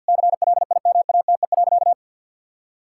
Morse code, heard as a keyed tone, about 50 wpm, 700 Hz. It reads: CLIMATE.
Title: Sentences from Top 1600 Words 50wpm